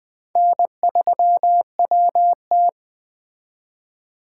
Morse code, heard as a keyed tone, about 21 wpm, 700 Hz. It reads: N3WT